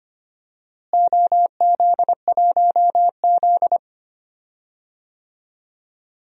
Morse code, heard as a keyed tone, about 25 wpm, 700 Hz. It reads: OZ1Z
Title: DXCC-100 Call Signs - 4 Characters - Repeated 2x - 25wpm